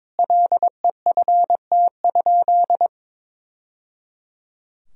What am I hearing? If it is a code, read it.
Morse code, heard as a keyed tone, about 22 wpm, 700 Hz. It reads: LEFT?
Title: Sentences from Top 1000 Words - Review Part 2 - 22wpm